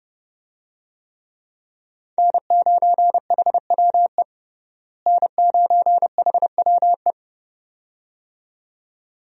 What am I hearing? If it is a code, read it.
Morse code, heard as a keyed tone, about 30 wpm, 700 Hz. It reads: N9HWE N9HWE
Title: USA Call Signs - 5 Characters - Repeated 2x - 30wpm